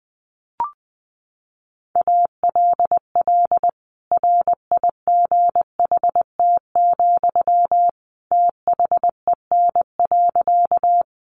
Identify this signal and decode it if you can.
Morse code, heard as a keyed tone, about 20 wpm, 700 Hz. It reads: ALL RIGHT, THEN.
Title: Sentences from Top 200 Words 20wpm